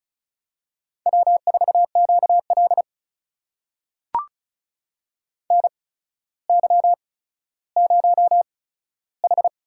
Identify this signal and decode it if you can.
Morse code, heard as a keyed tone, about 36 wpm, spaced as for 15 wpm, 700 Hz. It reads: W4QL NY0H
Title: USA Call Signs - 4 Characters - 8x Character Spacing - 35wpm